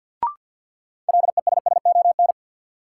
Morse code, heard as a keyed tone, about 53 wpm, 700 Hz. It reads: LESSON
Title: Sentences from Top 1200 Words - New Word Drill Alt - 50wpm